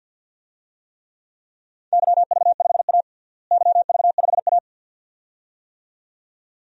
Morse code, heard as a keyed tone, about 50 wpm, 700 Hz. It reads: YV5A YV5A
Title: DXCC-100 Call Signs - 4 Characters - Repeated 2x - 50wpm